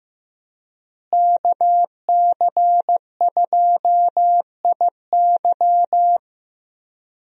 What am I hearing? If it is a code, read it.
Morse code, heard as a keyed tone, about 16 wpm, 700 Hz. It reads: KC2IY
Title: USA Call Signs - 5 Characters - Repeated 3x - 15wpm